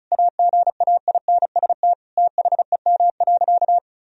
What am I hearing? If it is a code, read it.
Morse code, heard as a keyed tone, about 36 wpm, 700 Hz. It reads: AGAINST THEM.
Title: Sentences from Top 1000 Words - Review Part 4 - 35wpm